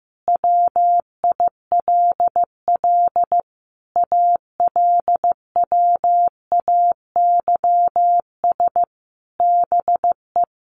Morse code, heard as a keyed tone, about 15 wpm, 700 Hz. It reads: WILL ALWAYS BE
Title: Sentences from Top 1500 Words 15wpm